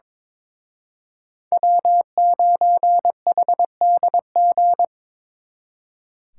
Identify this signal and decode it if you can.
Morse code, heard as a keyed tone, about 25 wpm, 700 Hz. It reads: W9HDG
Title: Speed Racing - USA 1x3 Call Signs 22wpm